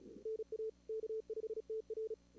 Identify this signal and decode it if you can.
Morse code, heard as a keyed tone, about 32 wpm, 440 Hz. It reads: NA K5TR
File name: CWSprint / Feb2022 / k5tr-feb-na-sprint-cw-k5ot -2300